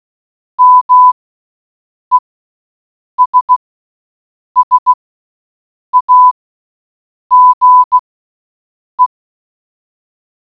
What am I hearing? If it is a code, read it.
Morse code, heard as a keyed tone, about 16 wpm, spaced as for 7 wpm, 1000 Hz. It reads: MESSAGE